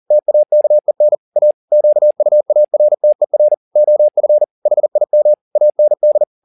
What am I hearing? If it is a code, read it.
Morse code, heard as a keyed tone, about 40 wpm, 600 Hz. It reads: TAKEN A QUARTER OF HIM AND
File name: the old man and the sea 2 40wpm